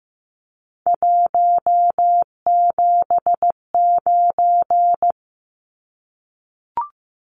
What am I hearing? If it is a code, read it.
Morse code, heard as a keyed tone, about 15 wpm, 700 Hz. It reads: 179
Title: Numbers - 3-Digit 15wpm